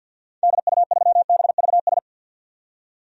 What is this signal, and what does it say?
Morse code, heard as a keyed tone, about 46 wpm, 700 Hz. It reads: DU3BVS
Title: DXCC-100 Call Signs - 6 Characters - 50wpm